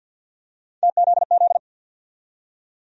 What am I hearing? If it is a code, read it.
Morse code, heard as a keyed tone, about 46 wpm, 700 Hz. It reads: TZZ
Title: USA Call Sign Postfixes - 3 Letters 50wpm